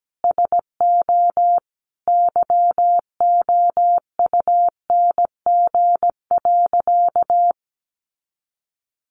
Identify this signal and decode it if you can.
Morse code, heard as a keyed tone, about 17 wpm, 700 Hz. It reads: SO YOUNG.
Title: Sentences from Top 400 Words 17wpm